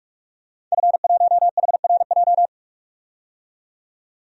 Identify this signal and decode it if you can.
Morse code, heard as a keyed tone, about 46 wpm, 700 Hz. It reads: F1HRJ
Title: DXCC-100 Call Signs - 5 Characters - 45wpm